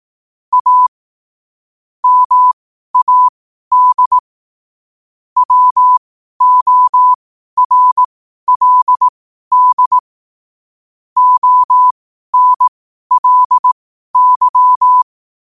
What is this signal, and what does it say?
Morse code, heard as a keyed tone, about 18 wpm, spaced as for 12 wpm, 1000 Hz. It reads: A MAD WORLD ONLY
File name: qotdfiles / 12wpm